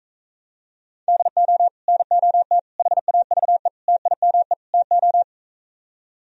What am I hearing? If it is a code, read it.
Morse code, heard as a keyed tone, about 45 wpm, 700 Hz. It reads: DO NOT HAVE TIME TO